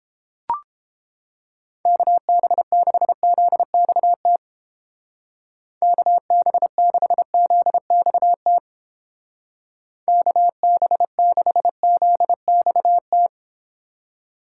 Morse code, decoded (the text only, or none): KB6ZXT KB6ZXT KB6ZXT